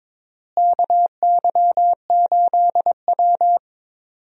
Morse code, heard as a keyed tone, about 21 wpm, 700 Hz. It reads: KY8W
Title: USA Call Signs - 4 Characters - Repeated 3x - 22wpm